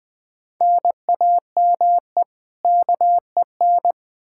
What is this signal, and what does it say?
Morse code, heard as a keyed tone, about 20 wpm, 700 Hz. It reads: NAME KEN
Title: Practice QSO - Exchange 1 - Beginning - 20wpm